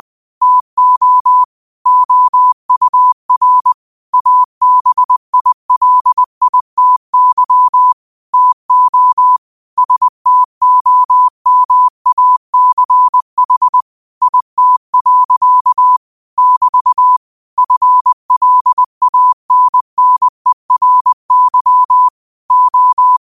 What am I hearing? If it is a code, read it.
Morse code, heard as a keyed tone, about 20 wpm, 1000 Hz. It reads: TO OUR ABILITY TO STOMACH IT. <BT> FLANNERY O